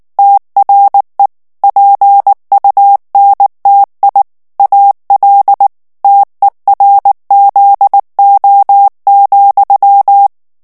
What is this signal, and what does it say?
Morse code, heard as a keyed tone, about 19 wpm, 800 Hz. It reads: TRE PUNTI AL TERZO,
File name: morse mp3 / ITALIEN texte clair 2 vitesse 18 mots minute